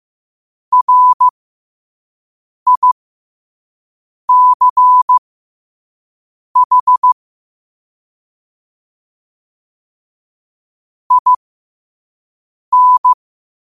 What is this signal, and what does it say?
Morse code, heard as a keyed tone, about 15 wpm, spaced as for 5 wpm, 1000 Hz. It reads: RICH IN
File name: qotdfiles / 5wpm